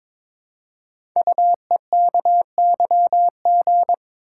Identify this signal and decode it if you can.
Morse code, heard as a keyed tone, about 22 wpm, 700 Hz. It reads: UEKYG
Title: Code Groups - 5 Letters - 22wpm